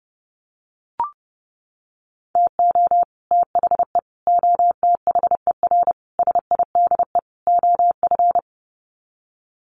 Morse code, heard as a keyed tone, about 30 wpm, 700 Hz. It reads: TO THE OTHER SIDE OF
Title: Sets of 5 Words Encore 30wpm